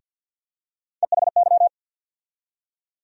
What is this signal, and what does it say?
Morse code, heard as a keyed tone, about 52 wpm, 700 Hz. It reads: EHY
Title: USA Call Sign Postfixes - 3 Letters 50wpm